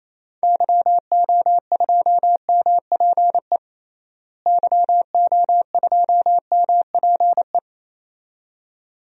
Morse code, heard as a keyed tone, about 28 wpm, 700 Hz. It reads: YO2MPE YO2MPE
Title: DXCC-100 Call Signs - 6 Characters - Repeated 2x - 28wpm